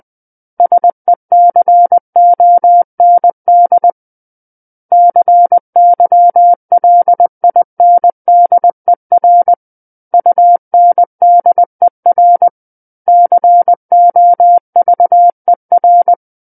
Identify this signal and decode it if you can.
Morse code, heard as a keyed tone, about 20 wpm, 700 Hz. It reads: SECOND CYLINDER UNDER COVER
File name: 20 WPM